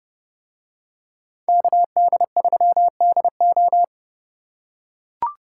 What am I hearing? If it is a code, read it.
Morse code, heard as a keyed tone, about 28 wpm, 700 Hz. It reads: KD3DO